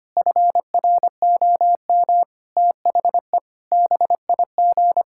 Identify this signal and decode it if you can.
Morse code, heard as a keyed tone, about 25 wpm, 700 Hz. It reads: FROM THE BIG
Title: Pangram Challenge v2 - 25wpm